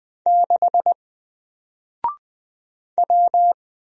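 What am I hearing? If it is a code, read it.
Morse code, heard as a keyed tone, about 20 wpm, 700 Hz. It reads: 6 W
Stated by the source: Single letter and number Morse drill at 20 wpm, from N4LCD.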